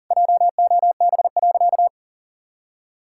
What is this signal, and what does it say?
Morse code, heard as a keyed tone, about 40 wpm, 700 Hz. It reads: JOB.
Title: Sentences from Top 1600 Words 40wpm